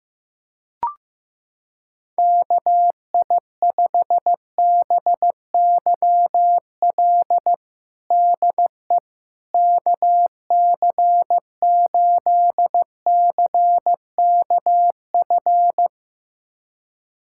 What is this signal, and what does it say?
Morse code, heard as a keyed tone, about 16 wpm, 700 Hz. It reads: KI5BYL DE KC8CKF
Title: USA Callsign DE Callsign - 6 Characters - 15wpm